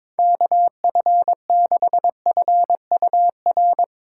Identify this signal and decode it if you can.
Morse code, heard as a keyed tone, about 24 wpm, 700 Hz. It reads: KF6FUR
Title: Speed Racing - USA 2x3 Call Signs 22wpm